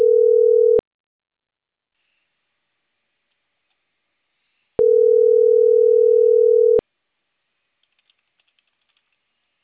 Ringback tone (North America), 440 + 480 Hz.